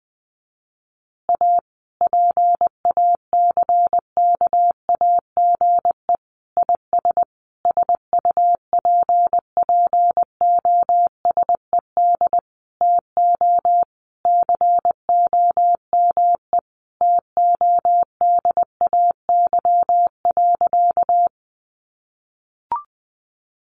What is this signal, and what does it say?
Morse code, heard as a keyed tone, about 20 wpm, 700 Hz. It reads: A PACKAGE IS SUPPOSED TO COME TODAY.
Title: Easy Phrases 20wpm